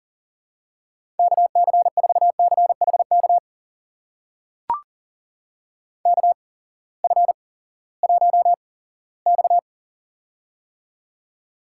Morse code, heard as a keyed tone, about 40 wpm, spaced as for 20 wpm, 700 Hz. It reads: KC4CHK KF1X
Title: USA Call Signs - 8x Character Spacing - 40wpm